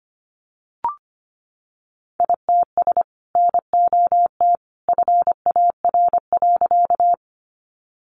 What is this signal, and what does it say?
Morse code, heard as a keyed tone, about 25 wpm, 700 Hz. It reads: ITS NOT FAR.
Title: Sentences from Top 600 Words 25wpm